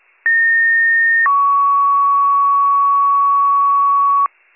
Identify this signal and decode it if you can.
Two-tone page, heard as a one-second tone, then a three-second tone, about 1850 then 1120 Hz, with noise. It is heard from 0.3 to 4.3 s.